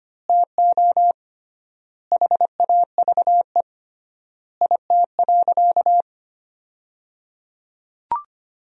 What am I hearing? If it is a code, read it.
Morse code, heard as a keyed tone, about 25 wpm, 700 Hz. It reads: TO HAVE IT.
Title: 3x Word-Spacing - Sentences from Top 100 Words - 25wpm